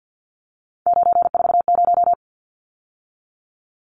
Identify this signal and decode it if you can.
Morse code, heard as a keyed tone, about 49 wpm, 700 Hz. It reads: 940
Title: Numbers - 3-Digit 50wpm